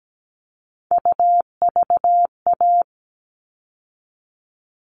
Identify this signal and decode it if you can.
Morse code, heard as a keyed tone, about 17 wpm, 700 Hz. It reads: UVA
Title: All Cut Numbers - 3-Digit 17wpm